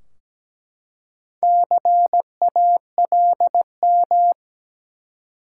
Morse code, heard as a keyed tone, about 17 wpm, 700 Hz. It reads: CALM
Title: Four Letter Words - 17wpm